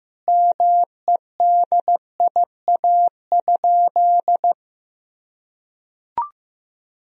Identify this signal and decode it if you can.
Morse code, heard as a keyed tone, about 16 wpm, 700 Hz. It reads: MEDIA?
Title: Sentences from Top 1000 Words - Review Part 4 - 15wpm